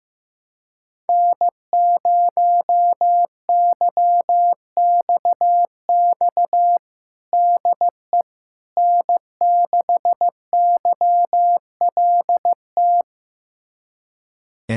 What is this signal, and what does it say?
Morse code, heard as a keyed tone, about 15 wpm, 700 Hz. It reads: N0YXX DE N6YLT
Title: USA Callsign DE Callsign - 5 Characters - 15wpm